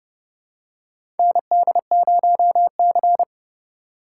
Morse code, heard as a keyed tone, about 30 wpm, 700 Hz. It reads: ND0C
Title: USA Call Signs - 4 Characters - 4x Character Spacing - 30wpm